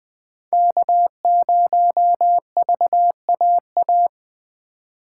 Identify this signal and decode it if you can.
Morse code, heard as a keyed tone, about 24 wpm, 700 Hz. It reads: K0VAA
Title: Speed Racing - USA 1x3 Call Signs 20wpm